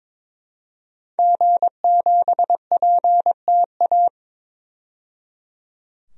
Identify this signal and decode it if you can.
Morse code, heard as a keyed tone, about 22 wpm, 700 Hz. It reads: G7PTA
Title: DXCC-100 Call Signs - 5 Characters - 22wpm